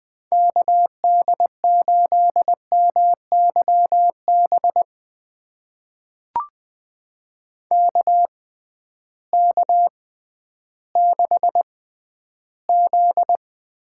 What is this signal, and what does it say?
Morse code, heard as a keyed tone, about 20 wpm, spaced as for 12 wpm, 700 Hz. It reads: KD8MYB KK6Z
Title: USA Call Signs - 6x Character Spacing - 20wpm